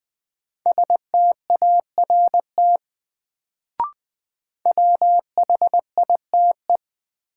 Morse code, heard as a keyed tone, about 20 wpm, 700 Hz. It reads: START WHITE
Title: Sentences from Top 300 Words - New Word Drill - 20wpm